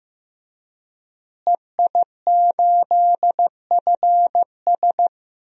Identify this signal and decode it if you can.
Morse code, heard as a keyed tone, about 16 wpm, 700 Hz. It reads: EI8FS
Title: DXCC-100 Call Signs - 5 Characters - Repeated 3x - 15wpm